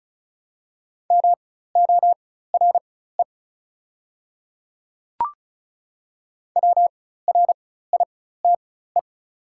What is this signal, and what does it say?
Morse code, heard as a keyed tone, about 36 wpm, spaced as for 16 wpm, 690 Hz. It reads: MORE WRITE
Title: 4x Character Spacing - Top 100 Words - 35wpm